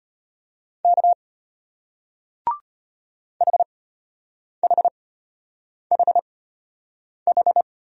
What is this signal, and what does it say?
Morse code, heard as a keyed tone, about 36 wpm, spaced as for 8 wpm, 700 Hz. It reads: K HHHH